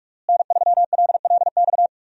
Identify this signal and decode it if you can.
Morse code, heard as a keyed tone, about 46 wpm, 700 Hz. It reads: N3LLX
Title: USA Call Signs - 5 Characters - Repeated 3x - 45wpm